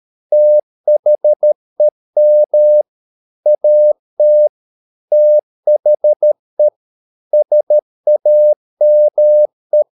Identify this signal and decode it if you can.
Morse code, heard as a keyed tone, about 13 wpm, 600 Hz. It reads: THEM AT THE SAME